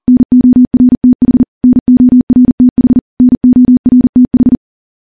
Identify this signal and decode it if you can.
Morse code, heard as a keyed tone, about 40 wpm, 250 Hz. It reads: NORTH NORTH NORTH
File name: morse / morse-training-40WPM